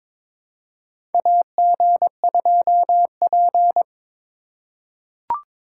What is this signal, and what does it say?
Morse code, heard as a keyed tone, about 22 wpm, 700 Hz. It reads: AG2P
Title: USA Call Signs - 4 Characters - 2x Character Spacing - 22wpm